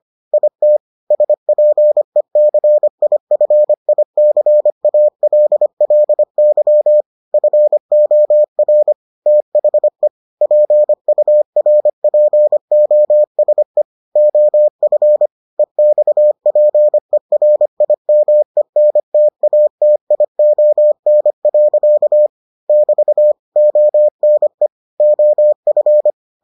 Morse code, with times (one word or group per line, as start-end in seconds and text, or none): 0.3-0.8 IT
1.1-7.0 SPECIFICALLY
7.3-8.9 FOR
9.3-10.1 THE
10.4-13.8 PURPOSE
14.2-15.3 OF
15.6-22.3 EXPERIMENTATION.
22.7-23.3 <BT>
23.6-24.7 ONE
25.0-26.1 OF